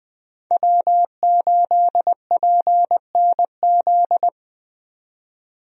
Morse code, heard as a keyed tone, about 20 wpm, 700 Hz. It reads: W8PNZ
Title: USA Call Signs - 5 Characters - 20wpm